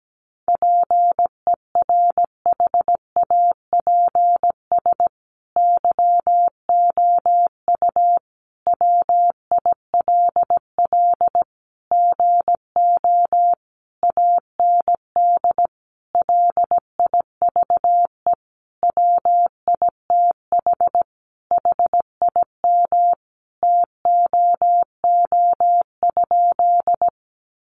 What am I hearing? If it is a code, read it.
Morse code, heard as a keyed tone, about 17 wpm, 700 Hz. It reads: PERHAPS YOU WILL GO AND LIVE WITH HIM TOO?
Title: Sentences from Top 500 Words 17wpm